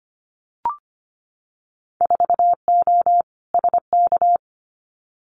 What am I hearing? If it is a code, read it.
Morse code, heard as a keyed tone, about 25 wpm, 690 Hz. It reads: 4O SK
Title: Winter Field Day Exchanges Encore 25wpm